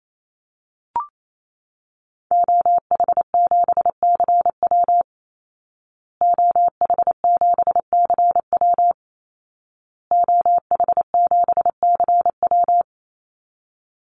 Morse code, heard as a keyed tone, about 28 wpm, 700 Hz. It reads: OH7CW OH7CW OH7CW
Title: Call Signs Repeated 2x - 28wpm